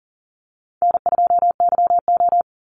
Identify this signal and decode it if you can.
Morse code, heard as a keyed tone, about 40 wpm, 700 Hz. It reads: N2YO
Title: Easy Call Signs Repeated 3x - 40wpm